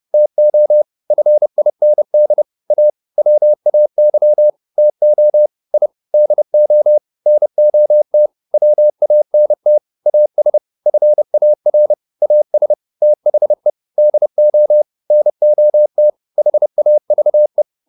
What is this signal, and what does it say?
Morse code, heard as a keyed tone, about 30 wpm, 600 Hz. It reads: TO FIND A WAY TO I DO NOT WANT AS FAR AS THE DO NOT HAVE